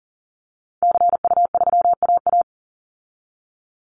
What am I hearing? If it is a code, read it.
Morse code, heard as a keyed tone, about 40 wpm, 700 Hz. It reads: CU3AA
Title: Call Signs Repeated 3x - 40wpm